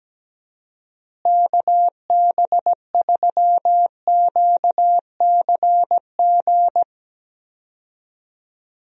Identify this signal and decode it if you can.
Morse code, heard as a keyed tone, about 17 wpm, 700 Hz. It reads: KB3QCG
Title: USA Call Signs - 6 Characters - Repeated 3x - 17wpm